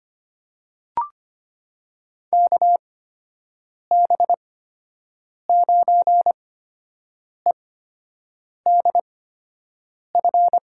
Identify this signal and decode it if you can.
Morse code, heard as a keyed tone, about 25 wpm, spaced as for 7 wpm, 700 Hz. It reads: KB9EDF
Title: USA Call Signs - 6 Characters - 8x Character Spacing - 25wpm